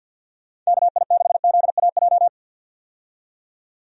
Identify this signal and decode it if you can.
Morse code, heard as a keyed tone, about 47 wpm, 700 Hz. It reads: KI6ZAJ
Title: USA Call Signs - 6 Characters - 50wpm